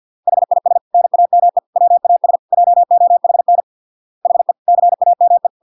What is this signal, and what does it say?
Morse code, heard as a keyed tone, about 50 wpm, 700 Hz. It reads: HIS NAME WAS JOHN HE CAME